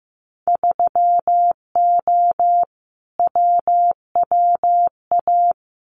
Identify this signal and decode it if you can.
Morse code, heard as a keyed tone, about 15 wpm, 700 Hz. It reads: 3O WWA
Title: Winter Field Day Exchanges Encore 15wpm